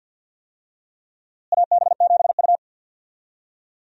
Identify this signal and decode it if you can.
Morse code, heard as a keyed tone, about 52 wpm, 700 Hz. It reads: AB7U
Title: USA Call Signs - 4 Characters - Repeated 3x - 50wpm